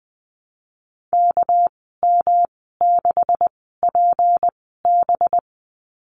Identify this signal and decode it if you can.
Morse code, heard as a keyed tone, about 20 wpm, spaced as for 14 wpm, 700 Hz. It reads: KM6PB
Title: 207 Lesson 33g - Letter j - Call Signs 20wpm